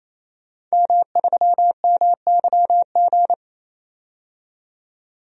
Morse code, heard as a keyed tone, about 28 wpm, 700 Hz. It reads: M3MYG
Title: DXCC-100 Call Signs - 5 Characters - 28wpm